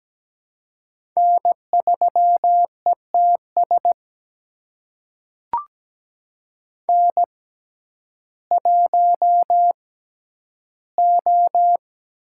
Morse code, heard as a keyed tone, about 17 wpm, spaced as for 10 wpm, 700 Hz. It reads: N3ETS N1O
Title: USA Call Signs - 5 Characters - 6x Character Spacing - 17wpm